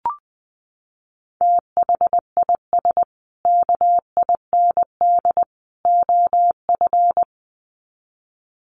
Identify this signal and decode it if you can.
Morse code, heard as a keyed tone, about 19 wpm, 700 Hz. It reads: THIS KIND OF
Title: Sets of 3 Words 20wpm